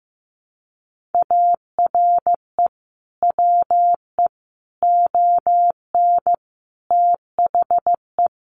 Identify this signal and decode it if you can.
Morse code, heard as a keyed tone, about 15 wpm, 700 Hz. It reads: ARE WE ON THE
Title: Sentences from Top 1400 Words 15wpm